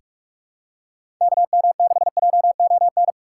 Morse code, heard as a keyed tone, about 42 wpm, 700 Hz. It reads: KM6JON
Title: USA Call Signs - 6 Characters - 2x Character Spacing - 45wpm